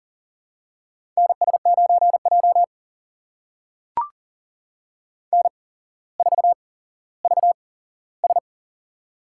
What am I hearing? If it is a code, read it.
Morse code, heard as a keyed tone, about 39 wpm, spaced as for 17 wpm, 700 Hz. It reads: NS9J N4VS
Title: USA Call Signs - 4 Characters - 8x Character Spacing - 40wpm